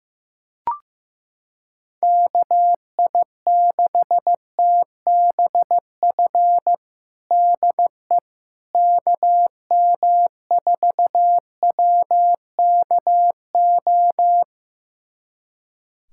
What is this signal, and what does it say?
Morse code, heard as a keyed tone, about 15 wpm, 700 Hz. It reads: KI6TBF DE KM4WKO
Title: USA Callsign DE Callsign 15wpm